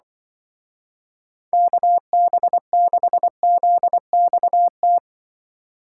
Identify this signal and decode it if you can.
Morse code, heard as a keyed tone, about 24 wpm, 700 Hz. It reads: KB6ZXT
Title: Speed Racing - USA 2x3 Call Signs 20wpm